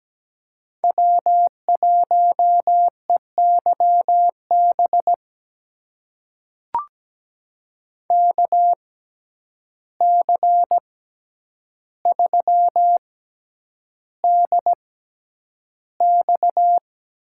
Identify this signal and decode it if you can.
Morse code, heard as a keyed tone, about 17 wpm, spaced as for 9 wpm, 700 Hz. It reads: W1EYB KC3DX
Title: USA Call Signs - 5 Characters - 6x Character Spacing - 17wpm